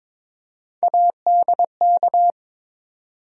Morse code, heard as a keyed tone, about 22 wpm, 700 Hz. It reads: ADK
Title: USA Call Sign Postfixes - 3 Letters 22wpm